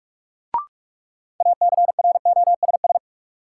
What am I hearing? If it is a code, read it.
Morse code, heard as a keyed tone, about 43 wpm, 700 Hz. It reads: ACROSS
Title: Sentences from Top 700 Words - New Word Drill - 45wpm